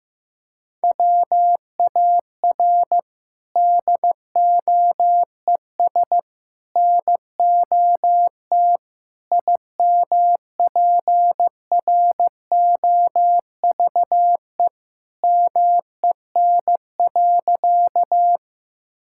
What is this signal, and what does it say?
Morse code, heard as a keyed tone, about 15 wpm, 700 Hz. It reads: WAR DOES NOT IMPROVE MEN.
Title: Sentences from Top 1000 Words - Review Part 5 - 15wpm